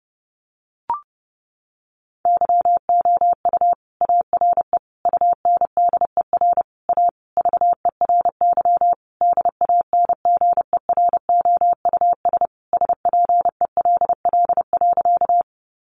Morse code, heard as a keyed tone, about 30 wpm, 700 Hz. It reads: YOU ARE UNDER A VERY DANGEROUS SPELL.
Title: Sentences from Top 1200 Words 30wpm